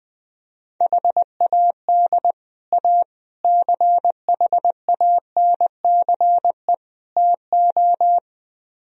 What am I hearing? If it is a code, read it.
Morse code, heard as a keyed tone, about 20 wpm, 700 Hz. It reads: HAD A CHANCE TO
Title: Speed-Racing - Sets of 4 Words 17wpm